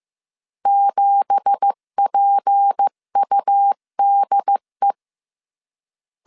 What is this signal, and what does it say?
Morse code, heard as a keyed tone, about 15 wpm, 800 Hz. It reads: 7PUDE
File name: CW / DET / 1 9 1500C S B 14wpm 3302